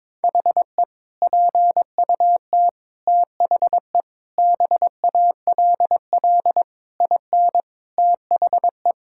Morse code, heard as a keyed tone, about 21 wpm, 700 Hz. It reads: HE PUT THE BALL IN THE
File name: practice / 22wpm / Sentences with 8-Word Phrases - 22wpm